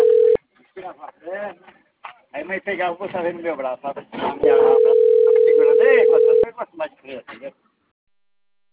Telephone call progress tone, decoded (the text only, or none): ringback tone